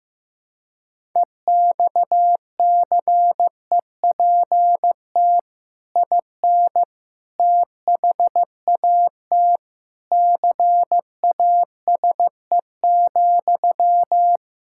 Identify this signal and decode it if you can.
Morse code, heard as a keyed tone, about 15 wpm, 700 Hz. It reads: EXCEPT IN THAT CASE,
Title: Sentences from Top 1600 Words 15wpm